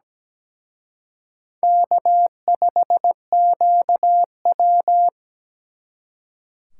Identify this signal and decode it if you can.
Morse code, heard as a keyed tone, about 20 wpm, 700 Hz. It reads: K5QW
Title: Speed Racing - USA 1x2 Call Signs 17wpm